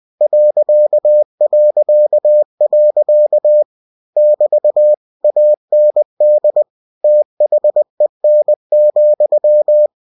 Morse code, heard as a keyed tone, about 20 wpm, 600 Hz. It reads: ... <BT> AND THEN,
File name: cw-books / winnie-the-pooh / winnie-20wpm-page-0019